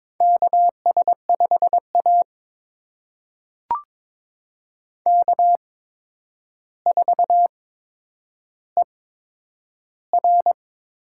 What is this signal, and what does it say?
Morse code, heard as a keyed tone, about 22 wpm, spaced as for 9 wpm, 700 Hz. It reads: KS5A K4ER
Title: USA Call Signs - 4 Characters - 8x Character Spacing - 22wpm